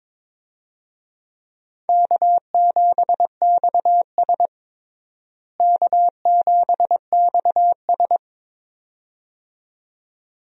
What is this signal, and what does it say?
Morse code, heard as a keyed tone, about 22 wpm, 700 Hz. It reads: K7XS K7XS
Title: USA Call Signs - 4 Characters - Repeated 2x - 22wpm